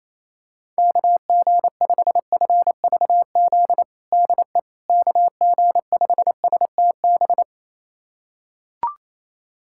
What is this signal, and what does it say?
Morse code, heard as a keyed tone, about 28 wpm, 700 Hz. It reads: KG5FVZ DE KG5STB
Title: USA Callsign DE Callsign - 6 Characters - 28wpm